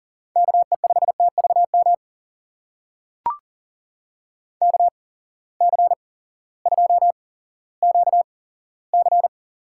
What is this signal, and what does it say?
Morse code, heard as a keyed tone, about 40 wpm, spaced as for 18 wpm, 700 Hz. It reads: KE5TVM KC2QC
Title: USA Call Signs - 6 Characters - 8x Character Spacing - 40wpm